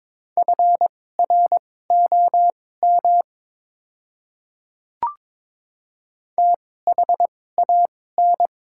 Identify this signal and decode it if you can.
Morse code, heard as a keyed tone, about 22 wpm, spaced as for 16 wpm, 700 Hz. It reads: FROM THAN
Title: 2x Character Spacing - Top 100 Words - 22wpm